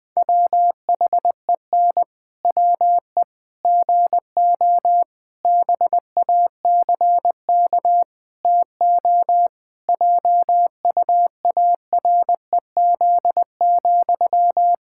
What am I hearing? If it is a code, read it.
Morse code, heard as a keyed tone, about 21 wpm, 700 Hz. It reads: WHEN WE GO BACK TO JUAREZ,
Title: Pangram Challenge v2 - 20wpm